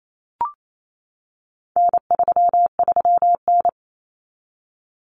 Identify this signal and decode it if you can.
Morse code, heard as a keyed tone, about 28 wpm, 700 Hz. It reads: N33N
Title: Common Cut Numbers - 4-Digit 28wpm